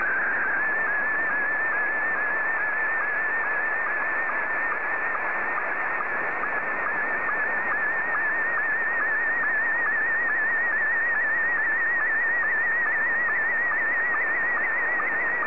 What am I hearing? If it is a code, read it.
SSTV (slow-scan television), Scottie 1.
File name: sstv / S1-VK7ZCF